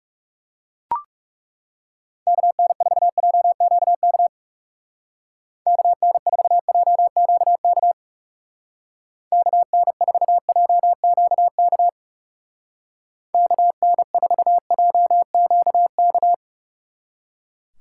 Morse code, heard as a keyed tone, about 39 wpm, 700 Hz. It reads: KN4JQK KN4JQK KN4JQK KN4JQK